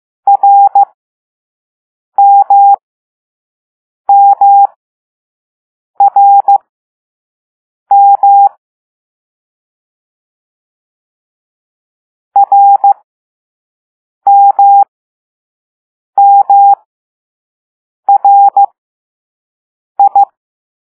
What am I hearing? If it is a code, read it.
Morse code, heard as a keyed tone, about 15 wpm, spaced as for 5 wpm, 800 Hz. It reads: RMMRM RMMRI
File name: Morse / Level2A